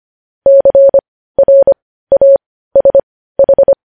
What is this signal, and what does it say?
Morse code, heard as a keyed tone, about 25 wpm, spaced as for 15 wpm, 550 Hz. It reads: CRASH
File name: podcast / News headlines-1x-25x15